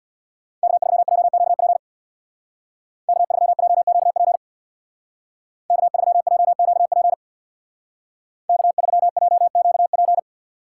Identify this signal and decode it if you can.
Morse code, heard as a keyed tone, about 61 wpm, 700 Hz. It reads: K3JQP K3JQP K3JQP K3JQP
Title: Speed Racing - USA 1x3 Call Signs 50wpm